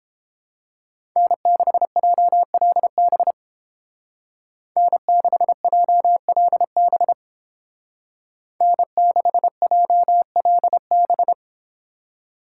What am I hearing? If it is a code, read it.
Morse code, heard as a keyed tone, about 29 wpm, 700 Hz. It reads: N6JLB N6JLB N6JLB